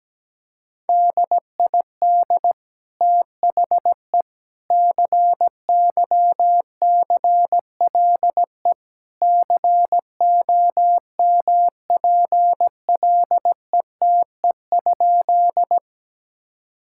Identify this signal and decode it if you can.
Morse code, heard as a keyed tone, about 17 wpm, 700 Hz. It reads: DID THE CYCLE COMPLETE?
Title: Sentences from Top 1000 Words - Review Part 6 - 17wpm